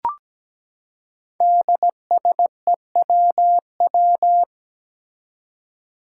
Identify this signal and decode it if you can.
Morse code, heard as a keyed tone, about 17 wpm, 700 Hz. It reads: DSEWW
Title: Five Challenging Letter-Number Combinations 17wpm